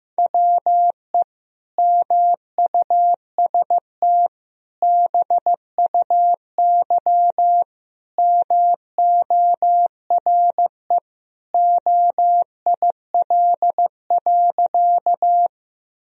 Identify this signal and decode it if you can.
Morse code, heard as a keyed tone, about 15 wpm, 700 Hz. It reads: WE MUST BUY MORE OIL.
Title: Sentences from Top 1000 Words - Review Part 4 - 15wpm